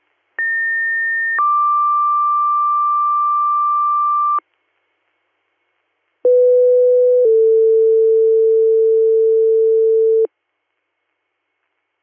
Two-tone page, heard as a one-second tone, then a three-second tone, about 1800 then 1180 Hz, about 500 then 450 Hz. It is heard from 0.4 to 10.3 s.